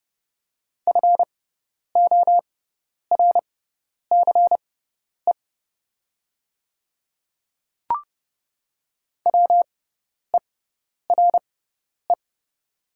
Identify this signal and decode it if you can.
Morse code, heard as a keyed tone, about 30 wpm, spaced as for 10 wpm, 700 Hz. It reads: FORCE WERE
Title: 6x Character Spacing - Top 500 Words - 30wpm